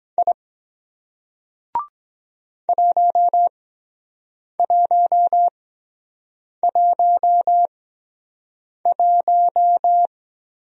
Morse code, heard as a keyed tone, about 23 wpm, spaced as for 7 wpm, 700 Hz. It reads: I 1111